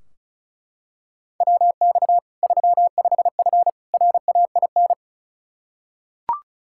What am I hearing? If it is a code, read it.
Morse code, heard as a keyed tone, about 34 wpm, 700 Hz. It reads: WX 35F RAIN
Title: K1USN SST QSO Practice 35wpm